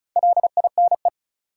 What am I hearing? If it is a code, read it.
Morse code, heard as a keyed tone, about 36 wpm, 700 Hz. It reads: LINE
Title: Sentences from Top 1000 Words - Review Part 1 - New Word Drill - 35wpm